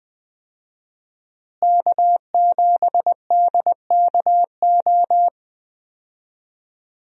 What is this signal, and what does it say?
Morse code, heard as a keyed tone, about 21 wpm, 700 Hz. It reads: K7DKO
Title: USA Call Signs - 5 Characters - Repeated 2x - 20wpm